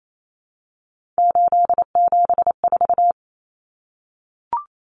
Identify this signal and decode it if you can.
Morse code, heard as a keyed tone, about 28 wpm, 700 Hz. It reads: 874
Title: Numbers - 3-Digit 28wpm